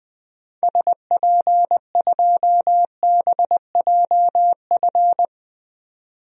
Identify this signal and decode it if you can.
Morse code, heard as a keyed tone, about 20 wpm, 700 Hz. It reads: SP2BJF
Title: DXCC-100 Call Signs - 6 Characters - 20wpm